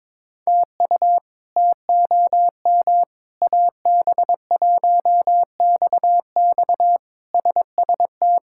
Morse code, heard as a keyed tone, about 22 wpm, 700 Hz. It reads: TU TOM AB1XX SST
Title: K1USN SST QSO Practice - 22wpm